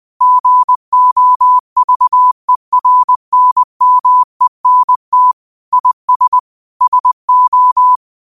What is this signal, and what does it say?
Morse code, heard as a keyed tone, about 20 wpm, 1000 Hz. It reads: GOVERNMENT IS SO
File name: qotdfiles / 20wpm